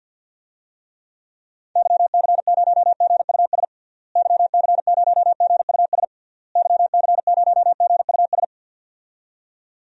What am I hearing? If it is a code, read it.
Morse code, heard as a keyed tone, about 52 wpm, 700 Hz. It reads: YC0GUS YC0GUS YC0GUS
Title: DXCC-100 Call Signs - 6 Characters - Repeated 3x - 50wpm